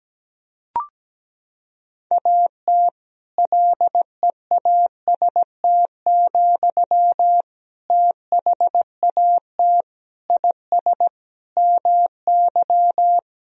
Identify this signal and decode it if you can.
Morse code, heard as a keyed tone, about 17 wpm, 700 Hz. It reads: AT LEAST, THAT IS MY
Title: Sentences from Top 1000 Words - Review Part 3 - 17wpm